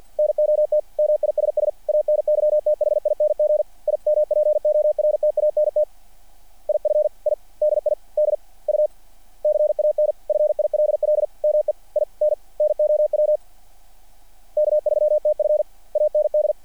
Morse code, heard as a keyed tone, about 48 wpm, 600 Hz. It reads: NOT MISS ANYTHING IMPORTANT IF I DID U CAN FILL ME IN NOW K3TF AND
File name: archived / CRSnet